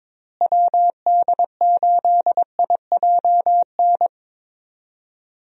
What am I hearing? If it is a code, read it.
Morse code, heard as a keyed tone, about 22 wpm, 700 Hz. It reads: WD8IJN
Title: USA Call Signs - 6 Characters - Repeated 2x - 22wpm